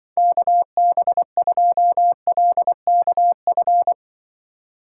Morse code, heard as a keyed tone, about 24 wpm, 700 Hz. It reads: KB2LKF
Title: Speed Racing - USA 2x3 Call Signs 20wpm